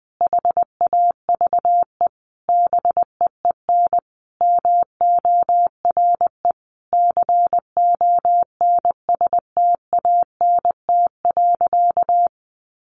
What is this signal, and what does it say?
Morse code, heard as a keyed tone, about 19 wpm, 700 Hz. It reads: HAVE BEEN MORE CONSTANT.
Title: Sentences from Top 1500 Words 20wpm